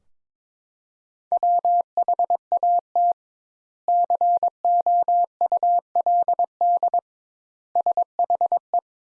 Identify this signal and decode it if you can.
Morse code, heard as a keyed tone, about 22 wpm, 700 Hz. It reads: WHAT COULD SHE
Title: Sentences Top 200 Words - 22wpm